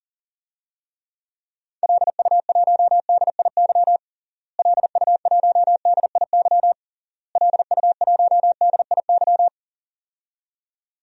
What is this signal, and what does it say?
Morse code, heard as a keyed tone, about 40 wpm, 700 Hz. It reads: LU1DIY LU1DIY LU1DIY
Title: DXCC-100 Call Signs - 6 Characters - Repeated 3x - 40wpm